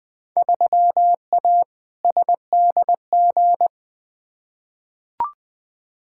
Morse code, heard as a keyed tone, about 20 wpm, 700 Hz. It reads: 3A SDG
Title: Field Day Exchanges - 20wpm